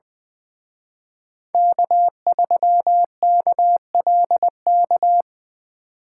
Morse code, heard as a keyed tone, about 24 wpm, 700 Hz. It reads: K3KLK
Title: Speed Racing - USA 1x3 Call Signs 20wpm